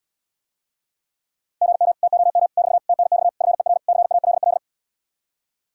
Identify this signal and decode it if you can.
Morse code, heard as a keyed tone, about 80 wpm, 700 Hz. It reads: GM TOM <BT> TNX FER CALL